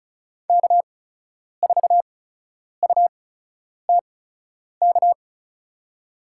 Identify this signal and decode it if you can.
Morse code, heard as a keyed tone, about 36 wpm, spaced as for 10 wpm, 700 Hz. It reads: K4UTK
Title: USA Call Signs - 5 Characters - 8x Character Spacing - 35wpm